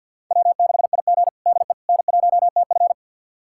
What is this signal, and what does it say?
Morse code, heard as a keyed tone, about 50 wpm, 690 Hz. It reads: W6IG DE N1TF
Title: USA Callsign DE Callsign - 4 Characters - 50wpm